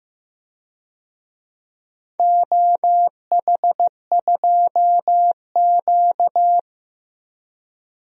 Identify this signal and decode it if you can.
Morse code, heard as a keyed tone, about 15 wpm, 700 Hz. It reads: OH2Q
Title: DXCC-100 Call Signs - 4 Characters - 15wpm